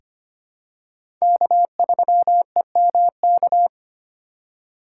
Morse code, heard as a keyed tone, about 25 wpm, 700 Hz. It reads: K3EMK
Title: USA Call Signs - 5 Characters - 3x Character Spacing - 25wpm